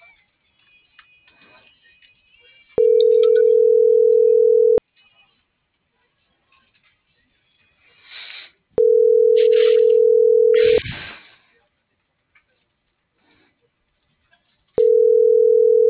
Ringback tone (North America), 440 + 480 Hz.